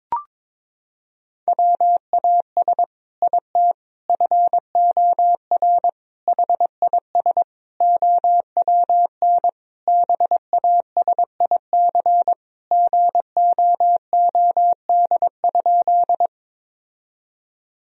Morse code, heard as a keyed tone, about 22 wpm, 700 Hz. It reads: WAS IT FOR HIS OWN BASIC GOOD?
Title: Sentences from Top 1000 Words - Review Part 3 - 22wpm